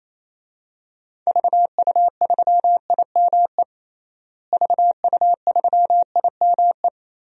Morse code, heard as a keyed tone, about 28 wpm, 700 Hz. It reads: VU3IME VU3IME
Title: DXCC-100 Call Signs - 6 Characters - Repeated 3x - 28wpm